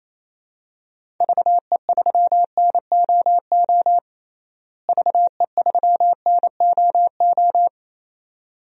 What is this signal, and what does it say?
Morse code, heard as a keyed tone, about 28 wpm, 700 Hz. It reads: VE3NOO VE3NOO
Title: DXCC-100 Call Signs - 6 Characters - Repeated 2x - 28wpm